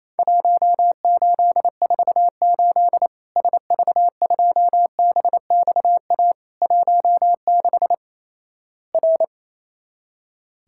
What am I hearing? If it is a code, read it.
Morse code, heard as a keyed tone, about 28 wpm, 700 Hz. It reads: 1848 SV2BXA 16 R
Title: Worked All Europe DX Contest - QTC Exchange - 28wpm